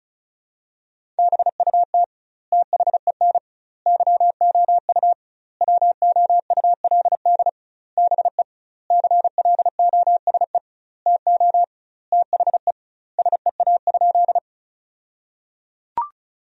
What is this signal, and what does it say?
Morse code, heard as a keyed tone, about 34 wpm, 700 Hz. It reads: BUT THEN YOU WOULD BE CLOSE TO THE SEA?
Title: 2x Word-Spacing - Sentences from Top 300 Words - 35wpm